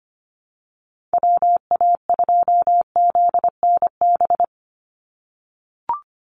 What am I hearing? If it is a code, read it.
Morse code, heard as a keyed tone, about 24 wpm, 700 Hz. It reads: WA2ZNB
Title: USA Call Signs 25wpm